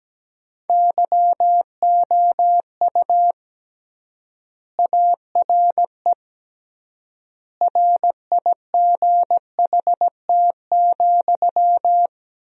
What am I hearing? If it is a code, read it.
Morse code, heard as a keyed tone, about 17 wpm, 700 Hz. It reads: YOU ARE RIGHT,
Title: 3x Word-Spacing - Sentences from Top 300 Words - 17wpm